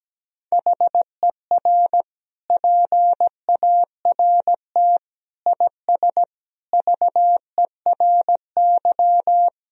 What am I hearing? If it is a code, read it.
Morse code, heard as a keyed tone, about 17 wpm, 700 Hz. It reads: HER PART IS VERY
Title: Sentences from Top 1000 Words - Review Part 1 - 17wpm